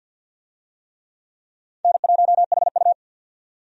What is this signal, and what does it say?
Morse code, heard as a keyed tone, about 47 wpm, 700 Hz. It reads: N1HU
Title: USA Call Signs - 4 Characters - 50wpm